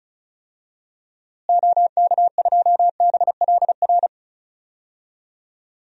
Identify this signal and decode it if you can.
Morse code, heard as a keyed tone, about 36 wpm, 700 Hz. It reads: OK2BLR
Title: DXCC-100 Call Signs - 6 Characters - Repeated 3x - 35wpm